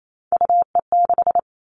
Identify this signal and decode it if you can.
Morse code, heard as a keyed tone, about 28 wpm, 700 Hz. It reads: UE6
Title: All Cut Numbers - 3-Digit 28wpm